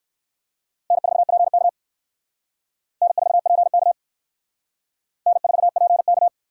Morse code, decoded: N4PK N4PK N4PK

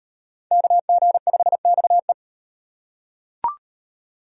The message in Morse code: KG5XE

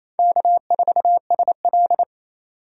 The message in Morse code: K4SL

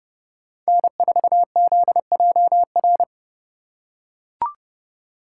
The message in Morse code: N4ZJR

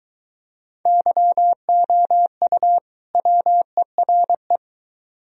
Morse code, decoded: YOU WERE

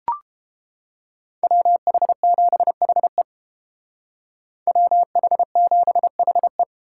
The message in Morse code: WH7HE WH7HE

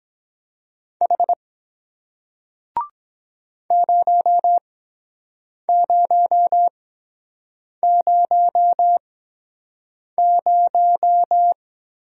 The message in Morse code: H 0000

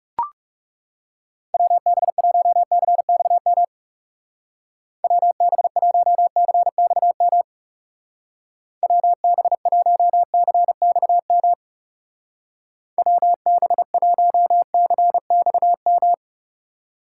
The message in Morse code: WB1CXM WB1CXM WB1CXM WB1CXM

